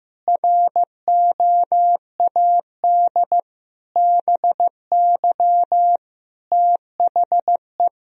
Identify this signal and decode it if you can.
Morse code, heard as a keyed tone, about 16 wpm, 700 Hz. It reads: ROAD BY THE
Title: Sentences from Top 1000 Words - Review Part 4 - 15wpm